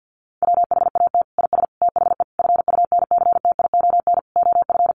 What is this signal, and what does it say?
Morse code, heard as a keyed tone, about 50 wpm, 700 Hz. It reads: WHAT IS THE FUNCTION OF
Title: Sentences from Top 1100 Words 50wpm